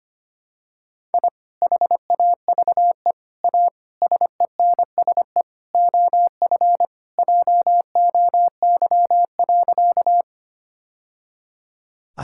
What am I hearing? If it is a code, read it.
Morse code, heard as a keyed tone, about 25 wpm, 700 Hz. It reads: I HAVE A SENSE OF JOY.
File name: practice / 25wpm / Sentences with 3-Word Phrases - 25wpm